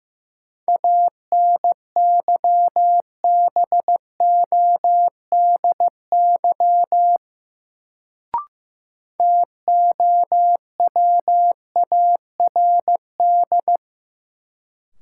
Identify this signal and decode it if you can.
Morse code, heard as a keyed tone, about 15 wpm, 700 Hz. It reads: ANYBODY TOWARD
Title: Sets of 5 Words - New Word Drill - 15wpm